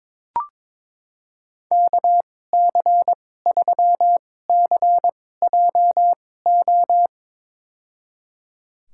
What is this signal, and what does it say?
Morse code, heard as a keyed tone, about 22 wpm, spaced as for 16 wpm, 700 Hz. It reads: KC3CJO
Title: USA Call Signs - 6 Characters - 2x Character Spacing - 22wpm